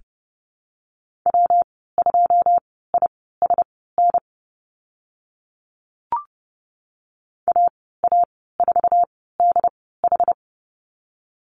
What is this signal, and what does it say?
Morse code, heard as a keyed tone, about 30 wpm, spaced as for 17 wpm, 700 Hz. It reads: W2ISN AA4DH
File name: Lesson 19g - Letter w - Call Signs 30wpm